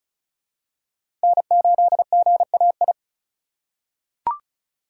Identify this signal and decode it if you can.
Morse code, heard as a keyed tone, about 36 wpm, 700 Hz. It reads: N8GAI
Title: USA Call Signs - 2x Character Spacing - 35wpm